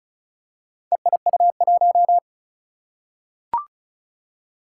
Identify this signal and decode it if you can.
Morse code, heard as a keyed tone, about 36 wpm, 700 Hz. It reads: EIU1
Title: Four Challenging Letter-Number Combinations 35wpm